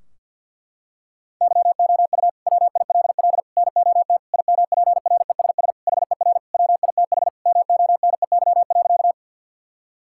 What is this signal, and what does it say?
Morse code, heard as a keyed tone, about 52 wpm, 700 Hz. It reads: YOU WILL NOT IMPRESS HER WITH MONEY.
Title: Sentences from Top 1600 Words 50wpm